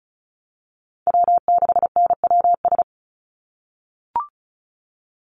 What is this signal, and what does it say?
Morse code, heard as a keyed tone, about 34 wpm, 700 Hz. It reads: W6NWS
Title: Call Signs - 35wpm